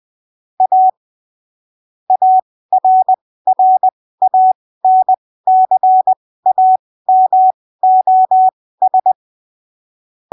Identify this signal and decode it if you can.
Morse code, heard as a keyed tone, about 20 wpm, spaced as for 15 wpm, 750 Hz. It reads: A ARRANCAMOS
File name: arquivos / texto / texto 20 20ppm